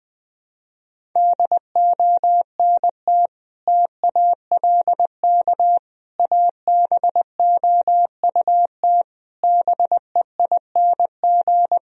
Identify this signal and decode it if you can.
Morse code, heard as a keyed tone, about 21 wpm, 700 Hz. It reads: DONT TALK ABOUT BEING
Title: Sentences from Top 1000 Words - Review Part 4 - 20wpm